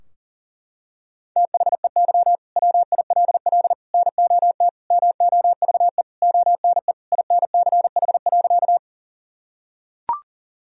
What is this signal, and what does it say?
Morse code, heard as a keyed tone, about 37 wpm, 700 Hz. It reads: THEY WILL NOT MOVE ONE INCH.
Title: Sentences from Top 1000 Words - Review Part 3 - 40wpm